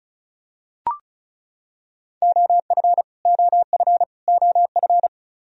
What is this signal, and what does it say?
Morse code, heard as a keyed tone, about 36 wpm, 700 Hz. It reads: OF OF OF
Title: Top 100 Words Repeated 35wpm